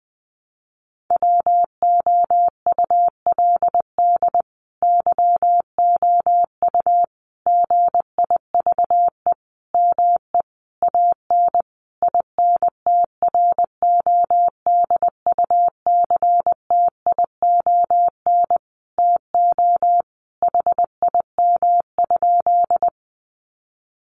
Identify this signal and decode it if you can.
Morse code, heard as a keyed tone, about 19 wpm, 700 Hz. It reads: WOULD YOU GIVE ME AN INTRODUCTION TO HIM?
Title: Sentences from Top 1000 Words 20wpm